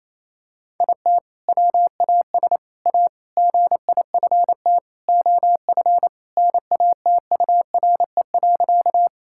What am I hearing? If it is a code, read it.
Morse code, heard as a keyed tone, about 27 wpm, 700 Hz. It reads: IT WAS A GIFT OF NATURE.
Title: Sentences from Top 1000 Words - Review Part 6 - 28wpm